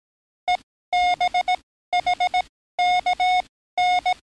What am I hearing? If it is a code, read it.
Morse code, heard as a keyed tone, about 18 wpm, spaced as for 13 wpm, 720 Hz. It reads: EBHKN
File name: training / NAVPERS 11049to11060 / NAVPERS 11056 RA RB